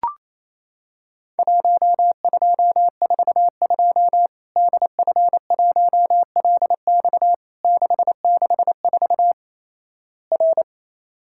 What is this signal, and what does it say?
Morse code, heard as a keyed tone, about 28 wpm, 700 Hz. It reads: 1242 DF1LX 664 R
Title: Worked All Europe DX Contest - QTC Exchange - 28wpm